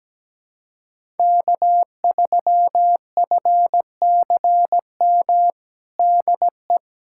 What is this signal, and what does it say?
Morse code, heard as a keyed tone, about 17 wpm, 700 Hz. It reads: K3FCM DE
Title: USA Callsign DE Callsign - 5 Characters - 17wpm